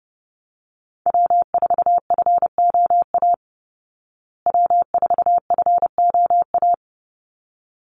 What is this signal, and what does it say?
Morse code, heard as a keyed tone, about 30 wpm, 700 Hz. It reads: W4FOA W4FOA
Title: Call Signs Repeated 3x - 30wpm